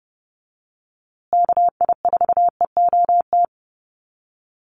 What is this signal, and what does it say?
Morse code, heard as a keyed tone, about 30 wpm, 700 Hz. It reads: KI4EOT
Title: USA Call Signs Repeated 3x 30wpm